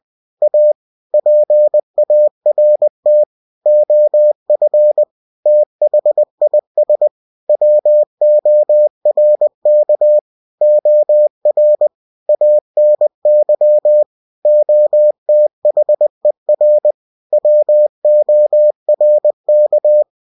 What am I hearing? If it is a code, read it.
Morse code, heard as a keyed tone, about 20 wpm, 600 Hz. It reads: A PART OF THIS WORK OR ANY OTHER WORK